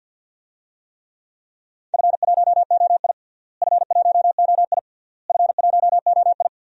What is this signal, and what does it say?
Morse code, heard as a keyed tone, about 47 wpm, 700 Hz. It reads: F1OI F1OI F1OI
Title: DXCC-100 Call Signs - 4 Characters - Repeated 3x - 50wpm